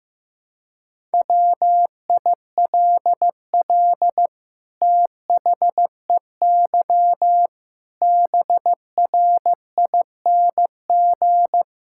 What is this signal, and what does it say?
Morse code, heard as a keyed tone, about 15 wpm, 700 Hz. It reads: WILL THEY BRING
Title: Sentences from Top 1000 Words - Review Part 6 - 15wpm